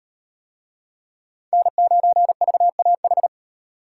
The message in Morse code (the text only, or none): N9VAH